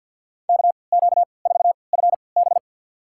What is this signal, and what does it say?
Morse code, heard as a keyed tone, about 51 wpm, spaced as for 28 wpm, 700 Hz. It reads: KQ4FB
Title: USA Call Signs - 5 Characters - 3x Character Spacing - 50wpm